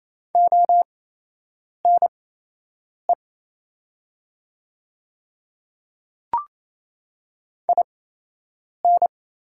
Morse code, heard as a keyed tone, about 28 wpm, spaced as for 8 wpm, 700 Hz. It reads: ONE IN
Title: 8x Character Spacing - Top 100 Words - 28wpm